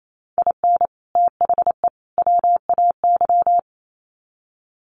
Morse code, heard as a keyed tone, about 28 wpm, 700 Hz. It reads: IN THE WAY
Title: Sets of 3 Words 28wpm